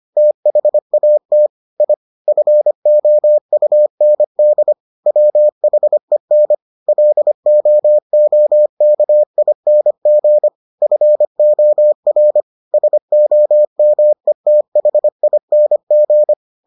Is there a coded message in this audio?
Yes: Morse code from 0.2 to 16.3 s, heard as a keyed tone, about 25 wpm, 600 Hz. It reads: THAT I FOUND WHEN LOOKING FOR SOMETHING